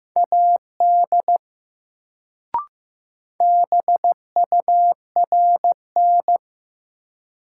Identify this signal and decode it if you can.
Morse code, heard as a keyed tone, about 15 wpm, 700 Hz. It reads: AD BURN
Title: Sentences from Top 1300 Words - New Word Drill Alt - 15wpm